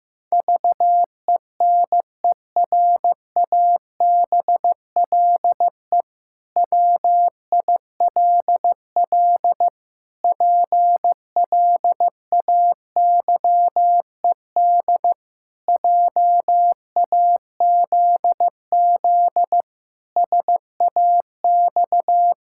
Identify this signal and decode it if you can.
Morse code, heard as a keyed tone, about 15 wpm, 700 Hz. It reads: VENERABLE WILL PLAYED JAZZ SAX
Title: Pangram Challenge v2 - 15wpm